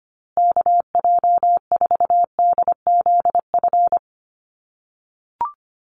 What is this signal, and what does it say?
Morse code, heard as a keyed tone, about 25 wpm, 700 Hz. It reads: KJ4DZF
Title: USA Call Signs Repeated 2x 25wpm